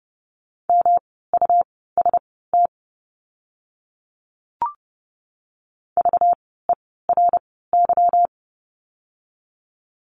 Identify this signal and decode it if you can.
Morse code, heard as a keyed tone, about 30 wpm, spaced as for 17 wpm, 700 Hz. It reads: MUST VERY
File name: course / 30:10wpm / 237 Lesson 38f - Letter q - Words 30wpm